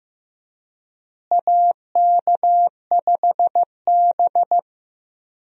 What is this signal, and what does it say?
Morse code, heard as a keyed tone, about 15 wpm, 700 Hz. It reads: AK5B
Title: USA Call Signs - 4 Characters - Repeated 2x - 15wpm